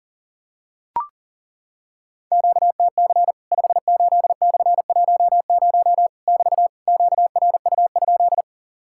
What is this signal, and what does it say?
Morse code, heard as a keyed tone, about 40 wpm, 700 Hz. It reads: QTC 58/10 <BT> QRU?
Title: Worked All Europe DX Contest - QTC Exchange - 40wpm